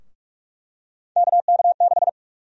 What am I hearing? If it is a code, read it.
Morse code, heard as a keyed tone, about 46 wpm, 700 Hz. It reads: KK6
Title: USA Call Sign Prefixes - 45wpm